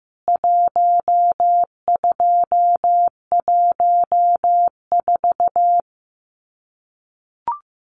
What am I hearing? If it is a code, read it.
Morse code, heard as a keyed tone, about 14 wpm, 700 Hz. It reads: 1214
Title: CWT Exchange 15wpm